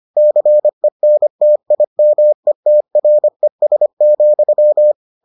Morse code, heard as a keyed tone, about 25 wpm, 600 Hz. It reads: CENTIMETRES,